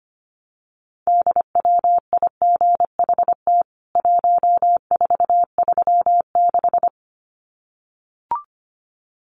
Morse code, heard as a keyed tone, about 25 wpm, 700 Hz. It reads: DWIGHT 1436
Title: CWT Exchange 25wpm